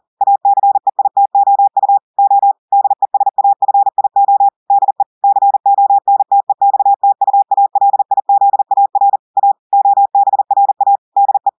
Morse code, heard as a keyed tone, about 40 wpm, 800 Hz. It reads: ACEITOU O DESAFIO DE CONTEXTUALIZAR A OBRA DE
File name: audios / portugais / portugais texte en clair 1 vitesse 40 mots minutes